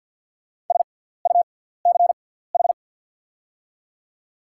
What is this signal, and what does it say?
Morse code, heard as a keyed tone, about 48 wpm, spaced as for 17 wpm, 700 Hz. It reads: SUCH
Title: 6x Character Spacing - Top 300 Words - 50wpm